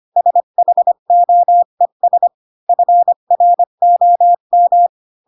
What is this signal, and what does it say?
Morse code, heard as a keyed tone, about 25 wpm, 700 Hz. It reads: SHOES FROM
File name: Pinocchio / Pinocchio ch8 20 25 ik1hge